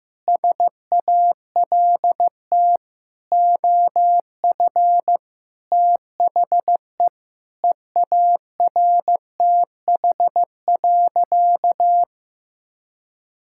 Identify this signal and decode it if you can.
Morse code, heard as a keyed tone, about 16 wpm, 700 Hz. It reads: SALT OF THE EARTH.
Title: Sentences from Top 1000 Words - Review Part 5 - 15wpm